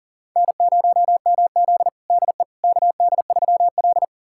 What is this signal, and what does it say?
Morse code, heard as a keyed tone, about 40 wpm, 700 Hz. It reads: N0MZ DE KD3L
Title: USA Callsign DE Callsign - 4 Characters - 40wpm